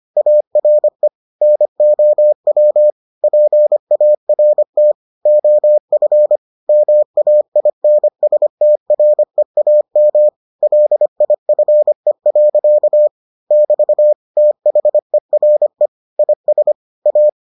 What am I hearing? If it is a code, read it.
Morse code, heard as a keyed tone, about 25 wpm, 600 Hz. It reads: ARE NOW PART OF MAINSTREAM LIFE. <BT> THERE IS A